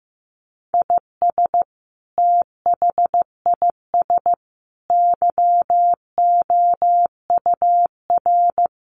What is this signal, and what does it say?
Morse code, heard as a keyed tone, about 15 wpm, 700 Hz. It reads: IS THIS YOUR
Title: Sentences from Top 1500 Words 15wpm